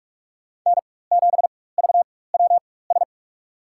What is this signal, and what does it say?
Morse code, heard as a keyed tone, about 44 wpm, spaced as for 21 wpm, 690 Hz. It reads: N7VWS
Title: USA Call Signs - 4x Character Spacing - 45wpm